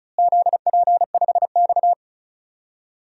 Morse code, heard as a keyed tone, about 34 wpm, 700 Hz. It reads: ZP5X
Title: DXCC-100 Call Signs - 4 Characters - Repeated 2x - 35wpm